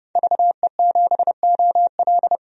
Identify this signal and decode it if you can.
Morse code, heard as a keyed tone, about 32 wpm, 700 Hz. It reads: VE7OL
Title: DXCC-100 Call Signs - 5 Characters - 30wpm